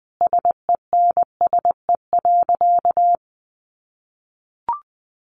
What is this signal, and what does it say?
Morse code, heard as a keyed tone, about 19 wpm, 700 Hz. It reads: SENSE.
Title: Sentences from Top 1100 Words 20wpm